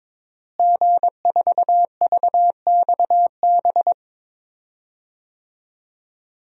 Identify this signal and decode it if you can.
Morse code, heard as a keyed tone, about 22 wpm, 700 Hz. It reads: G4VXB
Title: DXCC-100 Call Signs - 5 Characters - Repeated 2x - 22wpm